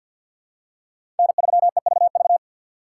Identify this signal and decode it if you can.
Morse code, heard as a keyed tone, about 52 wpm, 700 Hz. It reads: N3EVV
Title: USA Call Signs - 5 Characters - 4x Character Spacing - 50wpm